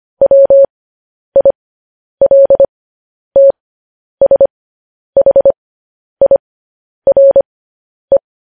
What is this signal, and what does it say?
Morse code, heard as a keyed tone, about 25 wpm, spaced as for 10 wpm, 550 Hz. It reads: WILTSHIRE